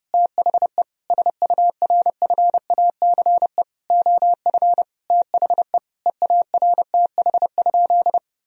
Morse code, heard as a keyed tone, about 30 wpm, 700 Hz. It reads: THE SURFACE OF THE EARTH?
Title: Sentences from Top 1000 Words - Review Part 3 - 30wpm